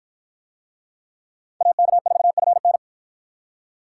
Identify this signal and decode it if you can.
Morse code, heard as a keyed tone, about 52 wpm, 700 Hz. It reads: AK4FN